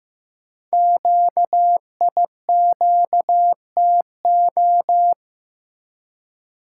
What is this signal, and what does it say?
Morse code, heard as a keyed tone, about 15 wpm, 700 Hz. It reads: QIQTO
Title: Code Groups - 5 Letters - 15wpm